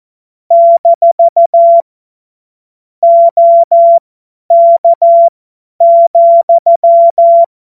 Morse code, heard as a keyed tone, about 14 wpm, spaced as for 10 wpm, 680 Hz. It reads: - OK,
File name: Witz-01-Juli-2025-50-F70